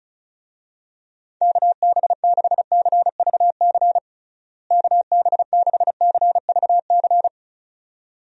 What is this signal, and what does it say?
Morse code, heard as a keyed tone, about 36 wpm, 700 Hz. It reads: KB6CVC KB6CVC
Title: USA Call Signs - 6 Characters - Repeated 2x - 35wpm